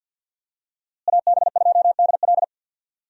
Morse code, heard as a keyed tone, about 47 wpm, 700 Hz. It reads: AB2DL
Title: USA Call Signs - 5 Characters - Repeated 2x - 50wpm